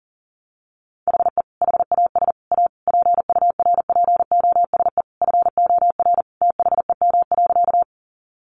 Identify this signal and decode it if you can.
Morse code, heard as a keyed tone, about 40 wpm, 700 Hz. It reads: HE HAS A PURPOSE FOR THEM.
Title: Sentences from Top 800 Words 40wpm